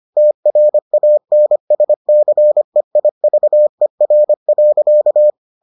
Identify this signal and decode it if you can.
Morse code, heard as a keyed tone, about 25 wpm, 600 Hz. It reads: TRANSCEIVER.